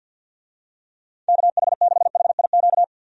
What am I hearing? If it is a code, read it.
Morse code, heard as a keyed tone, about 51 wpm, 700 Hz. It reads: KH6HIQ
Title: USA Call Signs - 6 Characters - 50wpm